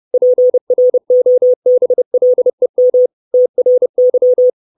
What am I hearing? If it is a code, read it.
Morse code, heard as a keyed tone, about 30 wpm, 500 Hz. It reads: PROBLEM TRY
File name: 632 Words 30WPM